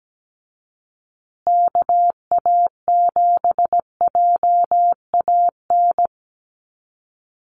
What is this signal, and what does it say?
Morse code, heard as a keyed tone, about 17 wpm, 700 Hz. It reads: KA7JAN
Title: USA Call Signs Repeated 2x 17wpm